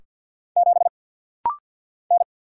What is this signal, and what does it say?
Morse code, heard as a keyed tone, about 49 wpm, 700 Hz. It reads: 7 N
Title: Single Letter-Number Alt - Rapid-Fire 50wpm